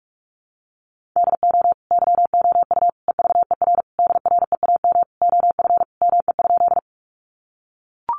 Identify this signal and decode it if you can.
Morse code, heard as a keyed tone, about 44 wpm, 700 Hz. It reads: DO YOU EVER DREAM OF ME?
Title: Sentences from Top 1500 Words 45wpm